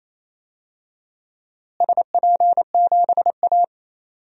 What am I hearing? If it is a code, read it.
Morse code, heard as a keyed tone, about 28 wpm, 700 Hz. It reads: SP7A
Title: DXCC-100 Call Signs - 4 Characters - Repeated 2x - 28wpm